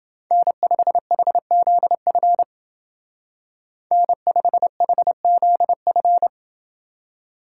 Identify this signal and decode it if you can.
Morse code, heard as a keyed tone, about 29 wpm, 700 Hz. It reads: N5HZF N5HZF